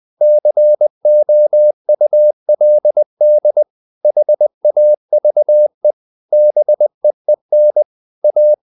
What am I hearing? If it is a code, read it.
Morse code, heard as a keyed tone, about 20 wpm, 600 Hz. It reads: COULD HAVE BEEN A
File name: the old man and the sea / the old man and the sea part 1 20wpm